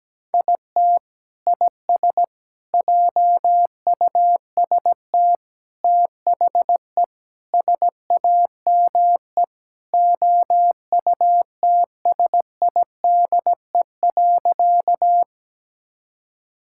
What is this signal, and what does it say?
Morse code, heard as a keyed tone, about 17 wpm, 700 Hz. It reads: IT IS JUST THE SAME OUTSIDE.
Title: Sentences from Top 1000 Words - Review Part 6 - 17wpm